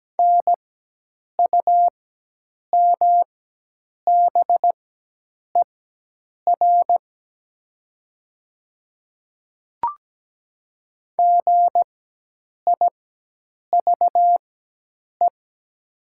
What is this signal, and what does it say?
Morse code, heard as a keyed tone, about 17 wpm, spaced as for 8 wpm, 700 Hz. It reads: NUMBER GIVE
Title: 4x Character Spacing - Top 200 Words - 17wpm